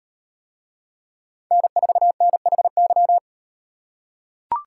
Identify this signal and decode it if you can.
Morse code, heard as a keyed tone, about 38 wpm, 700 Hz. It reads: N4NHY